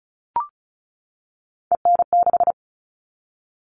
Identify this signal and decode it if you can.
Morse code, heard as a keyed tone, about 34 wpm, 700 Hz. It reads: EN6
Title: All Cut Numbers - 3-Digit 35wpm